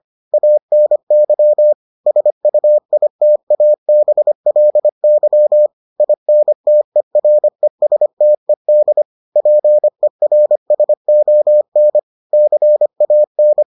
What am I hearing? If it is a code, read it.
Morse code, heard as a keyed tone, about 25 wpm, 600 Hz. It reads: ANY SUITABLY INTERESTED PERSON CAN